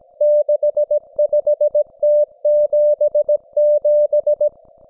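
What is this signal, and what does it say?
Morse code, heard as a keyed tone, about 17 wpm, 590 Hz. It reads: 65T77
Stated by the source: Russian military Morse station, M32, RDL, on 5268 kHz.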